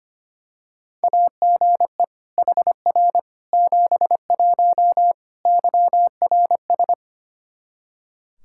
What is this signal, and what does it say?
Morse code, heard as a keyed tone, about 25 wpm, 700 Hz. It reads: AGE HR 71 YRS
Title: Practice QSO - Exchange 3 - More Info - 25wpm